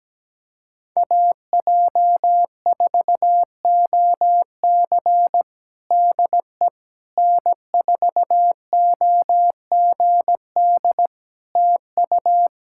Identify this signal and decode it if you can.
Morse code, heard as a keyed tone, about 17 wpm, 700 Hz. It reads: AJ4OC DE N4OGD TU
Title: Practice QSO - Exchange 4 - Ending - 17wpm